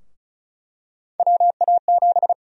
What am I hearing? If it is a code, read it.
Morse code, heard as a keyed tone, about 34 wpm, 700 Hz. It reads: WA7
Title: USA Call Sign Prefixes - 35wpm